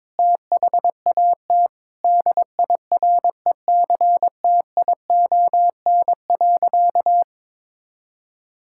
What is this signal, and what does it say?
Morse code, heard as a keyed tone, about 22 wpm, 700 Hz. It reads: THAT DIRECTION.
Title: Sentences from Top 1000 Words - Review Part 5 - 22wpm